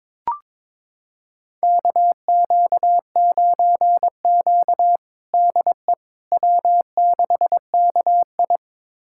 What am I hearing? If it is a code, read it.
Morse code, heard as a keyed tone, about 22 wpm, 700 Hz. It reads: KQ9Q DE W6KI
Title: USA Callsign DE Callsign - 4 Characters - 22wpm